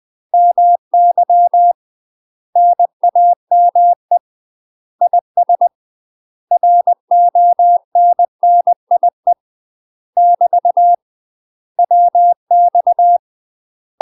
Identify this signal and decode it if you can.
Morse code, heard as a keyed tone, about 20 wpm, 700 Hz. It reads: MY NAME IS RONNIE <BT> WX
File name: UK MASTER QSOs TRAINING / 20WPM / UK MASTER QSOs 0053